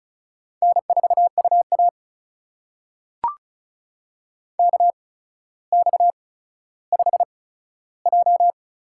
Morse code, heard as a keyed tone, about 36 wpm, spaced as for 15 wpm, 700 Hz. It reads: N4UA KX5J
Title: USA Call Signs - 4 Characters - 8x Character Spacing - 35wpm